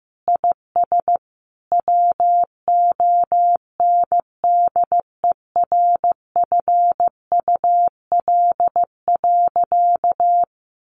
Morse code, heard as a keyed tone, about 15 wpm, 700 Hz. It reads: IS WONDERFUL.
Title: Sentences from Top 1100 Words 15wpm